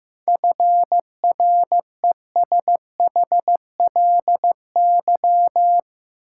Morse code, heard as a keyed tone, about 15 wpm, 700 Hz. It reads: FRESHLY
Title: Sets of 5 Words Encore - New Word Drill - 15wpm